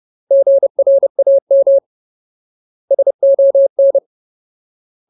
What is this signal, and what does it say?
Morse code, heard as a keyed tone, about 30 wpm, 550 Hz. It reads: GRAM SON
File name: Ultimate Word List / 30WPM Ultimate